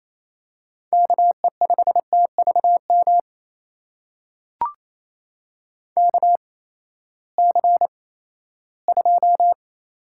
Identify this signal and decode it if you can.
Morse code, heard as a keyed tone, about 28 wpm, spaced as for 16 wpm, 700 Hz. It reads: KE5TVM KC2
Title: USA Call Signs - 6 Characters - 8x Character Spacing - 28wpm